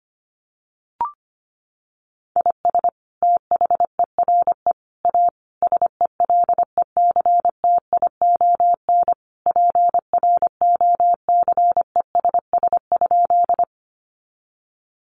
Morse code, heard as a keyed tone, about 25 wpm, 700 Hz. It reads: IS THERE A SELECTION PROCESS?
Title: Sentences from Top 1400 Words 25wpm